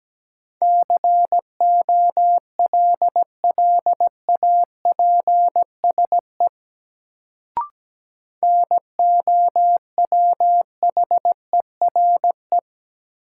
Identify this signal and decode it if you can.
Morse code, heard as a keyed tone, about 17 wpm, 700 Hz. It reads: COLLAPSE NOWHERE
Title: Sets of 5 Words Encore - New Word Drill - 17wpm